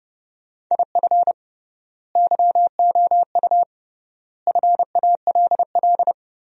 Morse code, heard as a keyed tone, about 30 wpm, 700 Hz. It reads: IF YOU FALL
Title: 3x Word-Spacing - Sentences from Top 500 Words - 30wpm